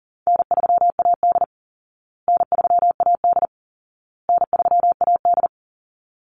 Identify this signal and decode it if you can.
Morse code, heard as a keyed tone, about 40 wpm, 700 Hz. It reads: N3AD N3AD N3AD
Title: Call Signs Repeated 2x - 40wpm